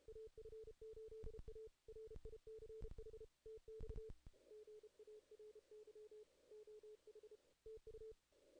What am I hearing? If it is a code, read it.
Morse code, heard as a keyed tone, about 32 wpm, 440 Hz. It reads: AF8A RICH TX GARY OH TU
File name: 2016 JAN CW NAQP